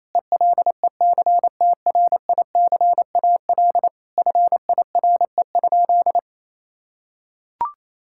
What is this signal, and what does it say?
Morse code, heard as a keyed tone, about 27 wpm, 700 Hz. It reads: ELECTRICAL FIRE?
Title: Sentences from Top 1000 Words - Review Part 6 - 28wpm